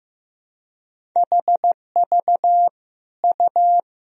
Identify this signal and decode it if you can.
Morse code, heard as a keyed tone, about 15 wpm, 700 Hz. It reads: HV U
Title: Sentences from Top 400 Words and Abbreviations 15wpm